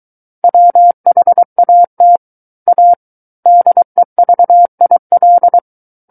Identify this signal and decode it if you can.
Morse code, heard as a keyed tone, about 23 wpm, 700 Hz. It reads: WHAT A DEVIL
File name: Book One / 14-Chapter 14 - 23 WPM